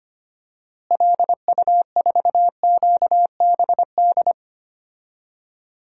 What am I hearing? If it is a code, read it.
Morse code, heard as a keyed tone, about 25 wpm, 700 Hz. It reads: LU4QBD
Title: DXCC-100 Call Signs - 6 Characters - Repeated 3x - 25wpm